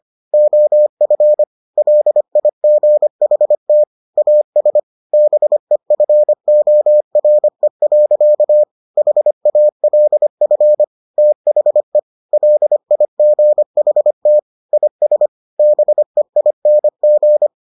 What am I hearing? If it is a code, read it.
Morse code, heard as a keyed tone, about 25 wpm, 600 Hz. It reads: OF LIGHT AS BEFORE. HALF THE LIGHT IS BEING